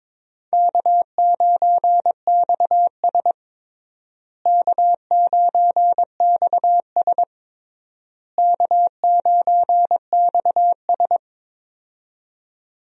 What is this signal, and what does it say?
Morse code, heard as a keyed tone, about 22 wpm, 700 Hz. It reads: K9XS K9XS K9XS
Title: USA Call Signs - 4 Characters - Repeated 3x - 22wpm